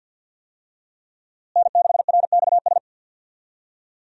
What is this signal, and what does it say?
Morse code, heard as a keyed tone, about 46 wpm, 700 Hz. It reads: N6RXS
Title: USA Call Signs - 5 Characters - Repeated 3x - 50wpm